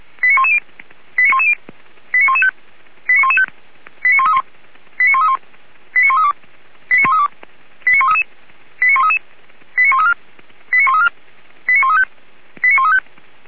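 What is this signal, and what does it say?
Five-tone selective call (ZVEI), heard as quick runs of five tones, tones of about 70 ms. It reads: 89110 89110 89115 89115 89131 89131 89132 89132 89133 89133 89134 89134 89135 89135
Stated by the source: ZVEI five-tone selective call 89227 received by SDR, Feuerwehr Orlamünde fire brigade.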